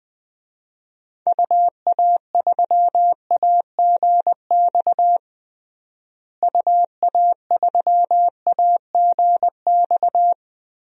Morse code, heard as a keyed tone, about 20 wpm, 700 Hz. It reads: UA3AGX UA3AGX
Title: DXCC-100 Call Signs - 6 Characters - Repeated 3x - 20wpm